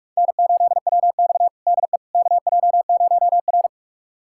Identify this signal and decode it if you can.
Morse code, heard as a keyed tone, about 43 wpm, 700 Hz. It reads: N8WX DE KJ0R
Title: USA Callsign DE Callsign - 4 Characters - 45wpm